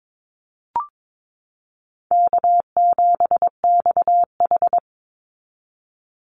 Morse code, heard as a keyed tone, about 22 wpm, 700 Hz. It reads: K7XH
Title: Call Signs Repeated 3x - 22wpm